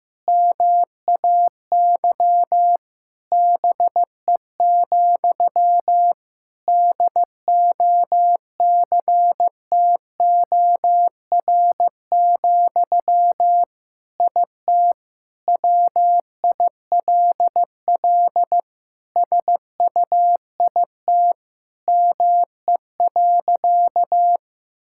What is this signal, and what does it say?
Morse code, heard as a keyed tone, about 15 wpm, 700 Hz. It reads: MAY BE, DOCTOR, IT WILL SUIT ME.
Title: Sentences from Top 1600 Words 15wpm